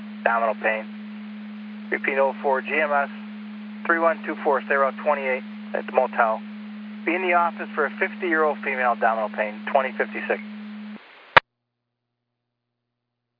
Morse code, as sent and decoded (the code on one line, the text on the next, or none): none